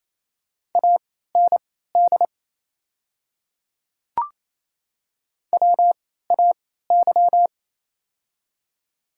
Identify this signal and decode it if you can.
Morse code, heard as a keyed tone, about 28 wpm, spaced as for 16 wpm, 700 Hz. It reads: AND WAY
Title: 3x Character Spacing - Top 100 Words - 28wpm